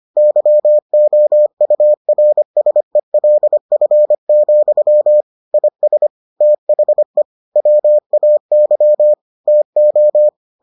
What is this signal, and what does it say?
Morse code, heard as a keyed tone, about 25 wpm, 600 Hz. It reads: YOURSELF, IS THE WAY TO